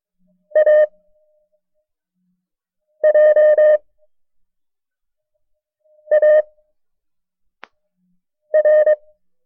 Morse code, heard as a keyed tone, about 23 wpm, spaced as for 4 wpm, 610 Hz. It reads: AJAR
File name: FreeFiles / CW-practice / KB6MT-tape1-side2